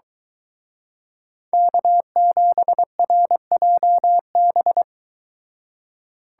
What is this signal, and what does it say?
Morse code, heard as a keyed tone, about 23 wpm, 700 Hz. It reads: K7RJB